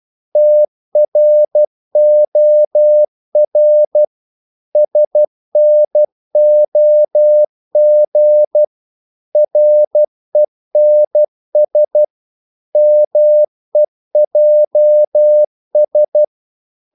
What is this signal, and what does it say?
Morse code, heard as a keyed tone, about 12 wpm, 600 Hz. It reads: TROR SNOG RENS MEJS
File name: Korte ord DK 12wpm